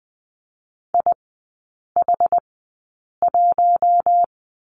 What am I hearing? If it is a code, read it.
Morse code, heard as a keyed tone, about 20 wpm, spaced as for 8 wpm, 700 Hz. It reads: IH1
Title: 074 Lesson 13e - Letter L - Three Characters 20wpm